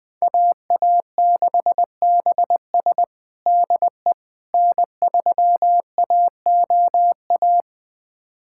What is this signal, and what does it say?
Morse code, heard as a keyed tone, about 20 wpm, 700 Hz. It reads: AA6BS DE N3AOA
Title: USA Callsign DE Callsign - 5 Characters - 20wpm